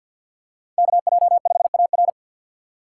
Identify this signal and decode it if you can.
Morse code, heard as a keyed tone, about 49 wpm, 690 Hz. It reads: KJ5AR
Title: USA Call Signs - 5 Characters - 2x Character Spacing - 50wpm